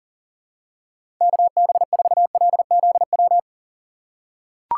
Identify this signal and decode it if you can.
Morse code, heard as a keyed tone, about 40 wpm, 700 Hz. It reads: KB4LZW
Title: USA Call Signs - 6 Characters - 3x Character Spacing - 40wpm